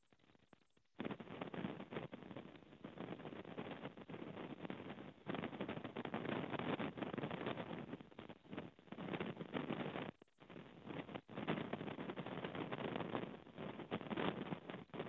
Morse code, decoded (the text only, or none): none